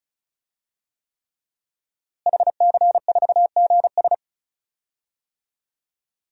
Morse code, heard as a keyed tone, about 36 wpm, 700 Hz. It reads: HC4GS
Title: DXCC-100 Call Signs - 5 Characters - Repeated 3x - 35wpm